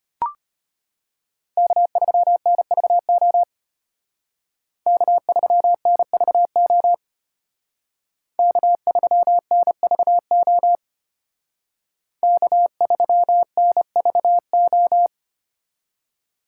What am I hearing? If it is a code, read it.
Morse code, heard as a keyed tone, about 32 wpm, 700 Hz. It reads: K3NVO K3NVO K3NVO K3NVO